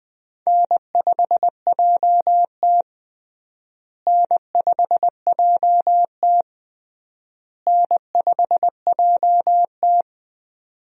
Morse code, heard as a keyed tone, about 20 wpm, 700 Hz. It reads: N5JT N5JT N5JT
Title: USA Call Signs - 4 Characters - Repeated 3x - 20wpm